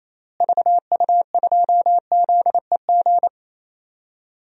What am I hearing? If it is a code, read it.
Morse code, heard as a keyed tone, about 28 wpm, 700 Hz. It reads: VU2ZEG
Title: DXCC-100 Call Signs - 6 Characters - Repeated 3x - 28wpm